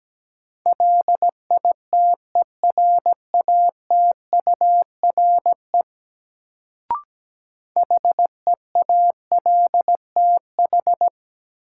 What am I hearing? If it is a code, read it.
Morse code, heard as a keyed tone, about 17 wpm, 700 Hz. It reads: LITERATURE HEALTH
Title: Sentences from Top 600 Words - New Word Drill - 17wpm